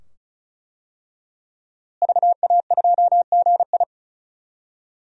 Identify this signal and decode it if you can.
Morse code, heard as a keyed tone, about 36 wpm, 700 Hz. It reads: VA2GI
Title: DXCC-100 Call Signs - 5 Characters - Repeated 2x - 35wpm